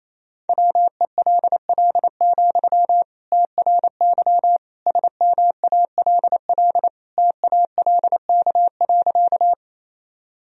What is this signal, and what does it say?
Morse code, heard as a keyed tone, about 28 wpm, 700 Hz. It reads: WELL, TRY SMALL TALK.
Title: Sentences from Top 1000 Words - Review Part 1 - 28wpm